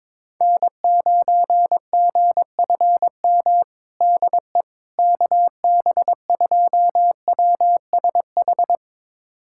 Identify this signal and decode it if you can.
Morse code, heard as a keyed tone, about 22 wpm, 700 Hz. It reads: N9GFM DE KB2WSH
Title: USA Callsign DE Callsign 22wpm